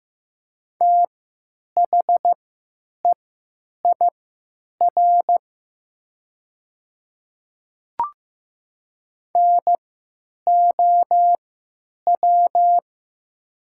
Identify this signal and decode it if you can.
Morse code, heard as a keyed tone, about 15 wpm, spaced as for 8 wpm, 700 Hz. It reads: THEIR NOW
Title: 3x Character Spacing - Top 100 Words - 15wpm